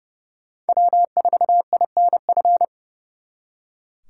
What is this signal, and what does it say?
Morse code, heard as a keyed tone, about 34 wpm, 700 Hz. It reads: W4INF
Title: Speed Racing - USA 1x3 Call Signs 30wpm